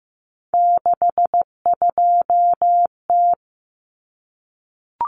Morse code, heard as a keyed tone, about 15 wpm, 700 Hz. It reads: 62T
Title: Common Cut Numbers - 3-Digit 15wpm